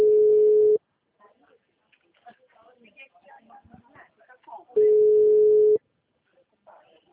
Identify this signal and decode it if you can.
Ringback tone (Europe, 425 Hz), with speech.